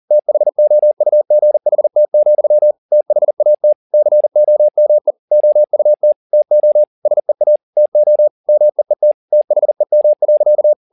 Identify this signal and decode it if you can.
Morse code, heard as a keyed tone, about 40 wpm, 600 Hz. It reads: THOUGHT, THAT COME OUT TO SEA TO MEET THEM.